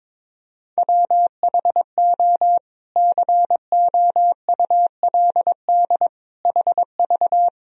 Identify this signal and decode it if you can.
Morse code, heard as a keyed tone, about 22 wpm, 700 Hz. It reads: WHO COULD HV
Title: Sentences from Top 300 Words and Abbreviations 22wpm